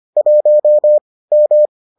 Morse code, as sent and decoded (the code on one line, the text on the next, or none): .---- / --
1 M